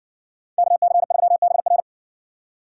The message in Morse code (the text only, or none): KC3BR